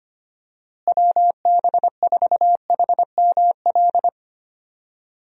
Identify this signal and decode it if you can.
Morse code, heard as a keyed tone, about 25 wpm, 700 Hz. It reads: WB4HML
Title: USA Call Signs - 6 Characters - Repeated 2x - 25wpm